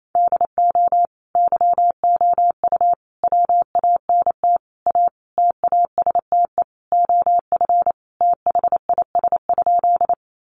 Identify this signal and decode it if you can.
Morse code, heard as a keyed tone, about 28 wpm, 700 Hz. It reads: DO YOU WANT A TASTE OF THIS?
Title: Sentences from Top 1100 Words 28wpm